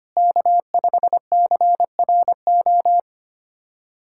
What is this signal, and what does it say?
Morse code, heard as a keyed tone, about 25 wpm, 700 Hz. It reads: K5CRO
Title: USA Callsign DE Callsign 25wpm